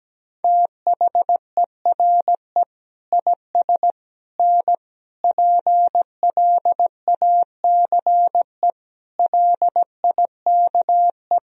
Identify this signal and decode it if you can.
Morse code, heard as a keyed tone, about 17 wpm, 700 Hz. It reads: THERE IS N PLACE LIKE
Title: Sentences from Top 200 Words and Abbreviations 17wpm